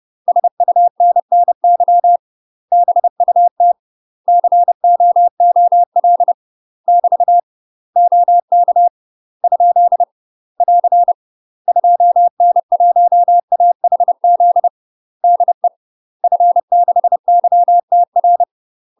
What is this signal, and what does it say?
Morse code, heard as a keyed tone, about 30 wpm, 700 Hz. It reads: SUNNY BUT COOL <BT> OK ? <AR> 2N1AHZ DE F6YTR
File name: UK MASTER QSOs TRAINING / 30WPM / UK MASTER QSOs 0002